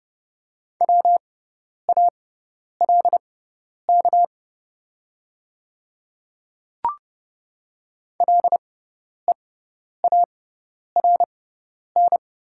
Morse code, heard as a keyed tone, about 30 wpm, spaced as for 10 wpm, 700 Hz. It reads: WALK LEARN
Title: 6x Character Spacing - Top 300 Words - 30wpm